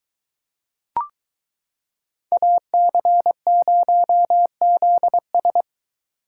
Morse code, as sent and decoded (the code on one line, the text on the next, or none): .- -.-. ----- --.. ...
AC0ZS